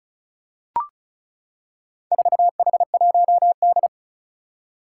Morse code, heard as a keyed tone, about 36 wpm, 700 Hz. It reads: 4H1D
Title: Four Challenging Letter-Number Combinations 35wpm